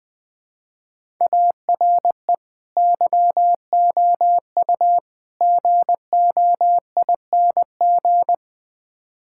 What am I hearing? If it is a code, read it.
Morse code, heard as a keyed tone, about 20 wpm, 700 Hz. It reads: ARE YOU GOING
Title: Build A Phrase - 4-Word Phrases - 20wpm